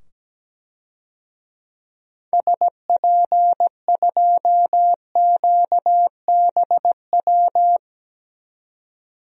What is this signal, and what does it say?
Morse code, heard as a keyed tone, about 17 wpm, 700 Hz. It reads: SP2QBW
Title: DXCC-100 Call Signs - 6 Characters - 17wpm